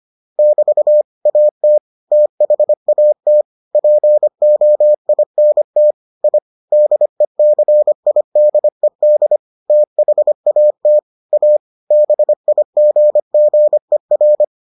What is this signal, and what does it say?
Morse code, heard as a keyed tone, about 25 wpm, 600 Hz. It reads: <BT> AT THAT POINT I DECIDED THAT A BIGGER